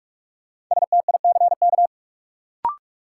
Morse code, heard as a keyed tone, about 46 wpm, 700 Hz. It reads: STICK
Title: Sentences from Top 1200 Words - New Word Drill - 45wpm